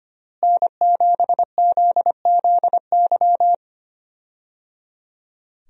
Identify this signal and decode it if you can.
Morse code, heard as a keyed tone, about 25 wpm, 700 Hz. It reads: N7ZZY
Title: USA Call Signs - 5 Characters - Repeated 3x - 25wpm